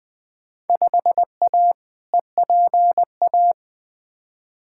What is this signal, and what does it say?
Morse code, heard as a keyed tone, about 21 wpm, 700 Hz. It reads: 5A EPA
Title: Field Day Exchanges - 20wpm